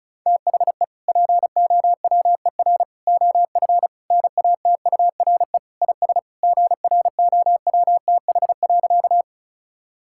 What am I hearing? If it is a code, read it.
Morse code, heard as a keyed tone, about 36 wpm, 700 Hz. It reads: THE POWER OF NATURE IS GROWTH.
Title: Sentences from Top 1000 Words - Review Part 5 - 35wpm